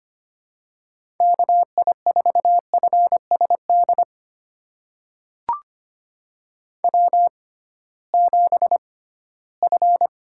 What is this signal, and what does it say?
Morse code, heard as a keyed tone, about 25 wpm, spaced as for 16 wpm, 700 Hz. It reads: KI4FSD W7F
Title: USA Call Signs - 6x Character Spacing - 25wpm